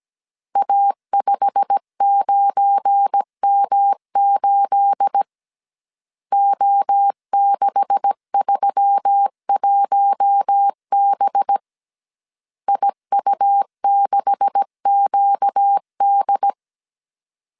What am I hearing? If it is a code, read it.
Morse code, heard as a keyed tone, about 17 wpm, 800 Hz. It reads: A59M8 O631B IU6QD
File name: CW / DET / 1 9 1500C S C 16wpm 2909